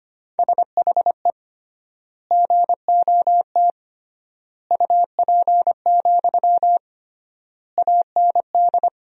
Morse code, heard as a keyed tone, about 25 wpm, 700 Hz. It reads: SHE GOT UP, AND
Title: 3x Word-Spacing - Sentences from Top 500 Words - 25wpm